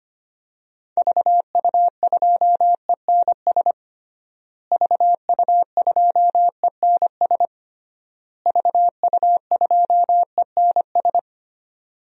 Morse code, heard as a keyed tone, about 25 wpm, 700 Hz. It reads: VU2ENS VU2ENS VU2ENS
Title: DXCC-100 Call Signs - 6 Characters - Repeated 3x - 25wpm